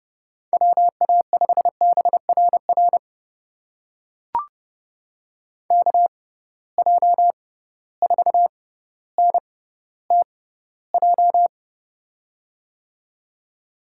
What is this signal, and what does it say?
Morse code, heard as a keyed tone, about 30 wpm, spaced as for 15 wpm, 700 Hz. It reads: WA5BRR KJ4NTJ
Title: USA Call Signs - 6x Character Spacing - 30wpm